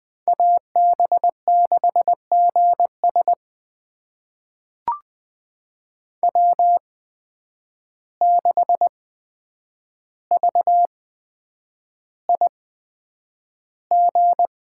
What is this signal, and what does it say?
Morse code, heard as a keyed tone, about 20 wpm, spaced as for 9 wpm, 700 Hz. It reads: AB6GS W6VIG
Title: USA Call Signs - 5 Characters - 8x Character Spacing - 20wpm